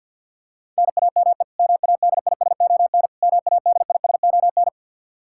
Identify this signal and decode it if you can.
Morse code, heard as a keyed tone, about 52 wpm, 700 Hz. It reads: NAME MADISON MADISON
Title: K1USN SST QSO Practice 50wpm